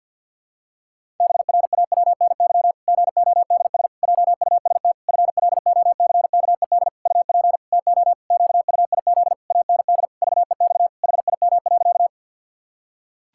Morse code, heard as a keyed tone, about 53 wpm, 700 Hz. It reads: BRAWNY GODS JUST FLOCKED UP TO QUIZ AND VEX HIM.
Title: Pangram Challenge v2 - 50wpm